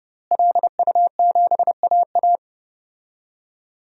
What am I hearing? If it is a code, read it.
Morse code, heard as a keyed tone, about 28 wpm, 700 Hz. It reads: LU7AA
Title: DXCC-100 Call Signs - 5 Characters - Repeated 3x - 30wpm